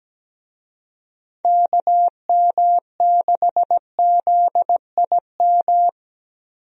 Morse code, heard as a keyed tone, about 17 wpm, 700 Hz. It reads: KM6ZIM
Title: USA Call Signs - 6 Characters - Repeated 2x - 17wpm